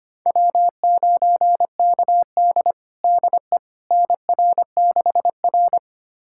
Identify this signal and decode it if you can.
Morse code, heard as a keyed tone, about 25 wpm, 700 Hz. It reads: W9KD DE NR6R
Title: USA Callsign DE Callsign - 4 Characters - 25wpm